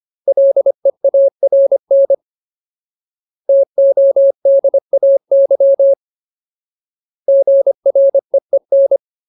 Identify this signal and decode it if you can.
Morse code, heard as a keyed tone, about 25 wpm, 550 Hz. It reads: LEARN TODAY GREEN
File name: Ultimate Word List / 25WPM Ultimate